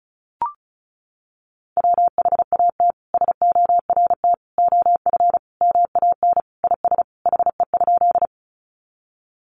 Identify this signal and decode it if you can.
Morse code, heard as a keyed tone, about 34 wpm, 700 Hz. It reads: WHAT SORT OF MAN IS HE?
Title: Sentences from Top 1300 Words 35wpm